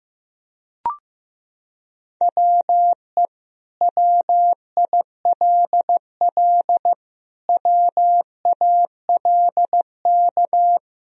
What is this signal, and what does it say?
Morse code, heard as a keyed tone, about 15 wpm, 700 Hz. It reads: WE WILL WALK
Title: Sentences from Top 1000 Words - Review Part 1 - 15wpm